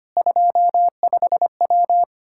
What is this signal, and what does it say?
Morse code, heard as a keyed tone, about 25 wpm, 700 Hz. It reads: 25W
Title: Practice QSO - Exchange 2 - Info - 25wpm